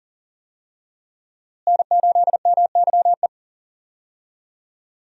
Morse code, heard as a keyed tone, about 38 wpm, 700 Hz. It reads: N8MYE
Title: USA Call Signs - 5 Characters - Repeated 2x - 40wpm